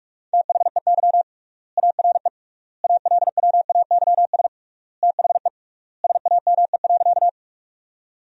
Morse code, heard as a keyed tone, about 43 wpm, 700 Hz. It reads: THEY ARE ALWAYS THE SAME.
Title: 3x Word-Spacing - Sentences from Top 300 Words - 45wpm